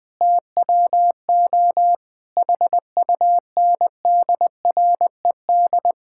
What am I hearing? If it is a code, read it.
Morse code, heard as a keyed tone, about 20 wpm, 700 Hz. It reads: TWO HUNDRED
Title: Sentences from Top 1000 Words - Review Part 5 - 20wpm